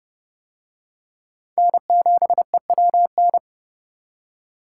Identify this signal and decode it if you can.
Morse code, heard as a keyed tone, about 29 wpm, 700 Hz. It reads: N7EWN